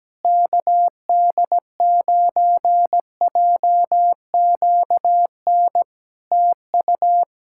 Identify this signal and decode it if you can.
Morse code, heard as a keyed tone, about 17 wpm, 700 Hz. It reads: KD9JQN TU
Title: Practice QSO - Exchange 4 - Ending - 17wpm